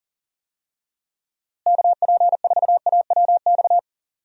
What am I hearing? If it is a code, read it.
Morse code, heard as a keyed tone, about 40 wpm, 700 Hz. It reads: KP4AWX
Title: DXCC-100 Call Signs - 6 Characters - Repeated 3x - 40wpm